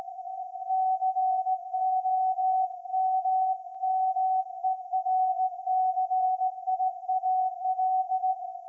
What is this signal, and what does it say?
Morse code, heard as a keyed tone, about 16 wpm, 750 Hz. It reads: COMMERCIAL
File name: iz7slz / VLF / SAQ 30-06-2024 0900